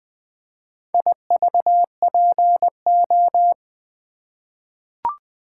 Morse code, heard as a keyed tone, about 20 wpm, 700 Hz. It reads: IVPO
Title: Four Letter-Number Combinations - 20wpm